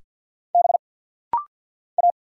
Morse code, heard as a keyed tone, about 50 wpm, 690 Hz. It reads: B A
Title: Single Letter-Number Alt - Rapid-Fire 50wpm